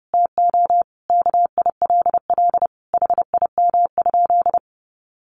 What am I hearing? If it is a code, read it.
Morse code, heard as a keyed tone, about 30 wpm, 700 Hz. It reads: TO KILL HIM?
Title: Sentences from Top 1400 Words 30wpm